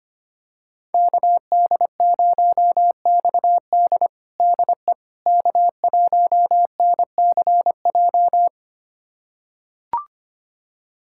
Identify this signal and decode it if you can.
Morse code, heard as a keyed tone, about 25 wpm, 700 Hz. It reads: KD0XD DE K1NCJ
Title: USA Callsign DE Callsign - 5 Characters - 25wpm